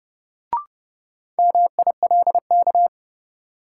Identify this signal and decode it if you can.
Morse code, heard as a keyed tone, about 30 wpm, 700 Hz. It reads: MILK
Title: Sentences from Top 1500 Words - New Word Drill Alt - 30wpm